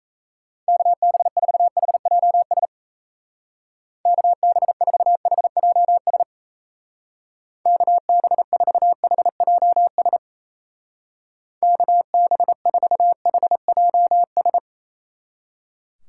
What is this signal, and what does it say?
Morse code, heard as a keyed tone, about 32 wpm, 700 Hz. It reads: KB4HJS KB4HJS KB4HJS KB4HJS
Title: Speed Racing - USA 2x3 Call Signs 28wpm